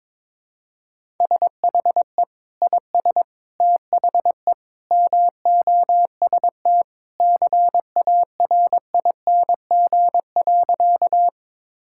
Morse code, heard as a keyed tone, about 22 wpm, 700 Hz. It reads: SHE IS THE MOST CARING.
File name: practice / 22wpm / Sentences with 2-Word Phrases - 22wpm